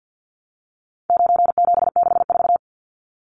Morse code, heard as a keyed tone, about 50 wpm, 700 Hz. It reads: 9764
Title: Numbers - 4-Digit 50wpm